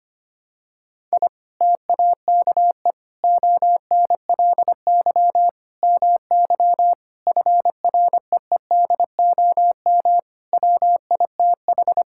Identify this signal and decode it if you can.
Morse code, heard as a keyed tone, about 25 wpm, 700 Hz. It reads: I TAKE ONLY MY FREEDOM WITH
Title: Sentences from Top 1000 Words - Review Part 5 - 25wpm